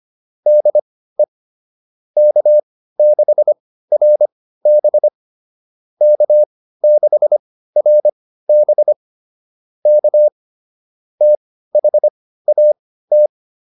Morse code, heard as a keyed tone, about 25 wpm, spaced as for 15 wpm, 600 Hz. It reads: DE K6RB K6RB K THAT